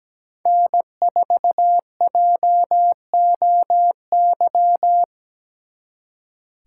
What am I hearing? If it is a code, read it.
Morse code, heard as a keyed tone, about 20 wpm, 700 Hz. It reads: N4JOY
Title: Speed Racing - USA 1x3 Call Signs 17wpm